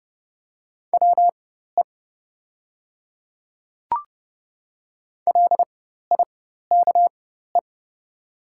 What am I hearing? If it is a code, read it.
Morse code, heard as a keyed tone, about 30 wpm, spaced as for 14 wpm, 700 Hz. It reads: WE LIKE
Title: 4x Character Spacing - Top 200 Words - 30wpm